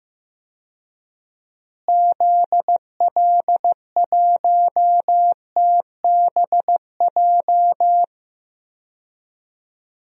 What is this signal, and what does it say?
Morse code, heard as a keyed tone, about 15 wpm, 700 Hz. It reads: ZL1TBJ
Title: DXCC-100 Call Signs - 6 Characters - Repeated 2x - 15wpm